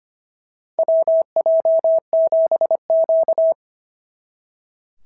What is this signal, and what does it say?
Morse code, heard as a keyed tone, about 25 wpm, 650 Hz. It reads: WJ7Q
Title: SOTA QSO Practice 25wpm